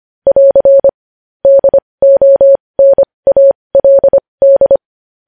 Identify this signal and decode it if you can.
Morse code, heard as a keyed tone, about 25 wpm, spaced as for 20 wpm, 550 Hz. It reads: <AR> DONALD
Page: News Headlines in Morse Code at 20 WPM